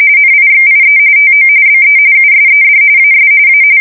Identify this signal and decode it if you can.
RTTY, about 45.45 baud. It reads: NIXIE TO BE USED AS A
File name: fsk170 / USS-Missouri